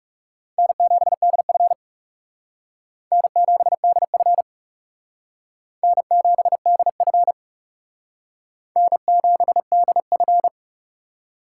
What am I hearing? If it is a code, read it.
Morse code, heard as a keyed tone, about 36 wpm, 700 Hz. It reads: N7DF N7DF N7DF N7DF
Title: Speed Racing - USA 1x2 Call Signs 30wpm